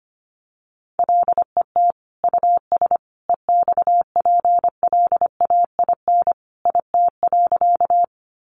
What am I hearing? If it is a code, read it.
Morse code, heard as a keyed tone, about 25 wpm, 700 Hz. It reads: LET US EXPLAIN IT.
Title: Sentences from Top 1100 Words 25wpm